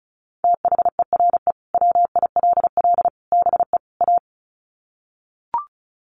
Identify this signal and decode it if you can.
Morse code, heard as a keyed tone, about 34 wpm, 700 Hz. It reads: THERE WILL BE A
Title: Sets of 4 Words 35wpm